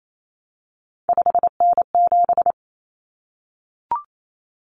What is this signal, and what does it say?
Morse code, heard as a keyed tone, about 28 wpm, 700 Hz. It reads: 5N7
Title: Common Cut Numbers - 3-Digit 28wpm